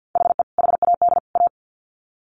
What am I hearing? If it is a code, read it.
Morse code, heard as a keyed tone, about 48 wpm, 700 Hz. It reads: HE HAD A
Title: Sets of 3 Words 50wpm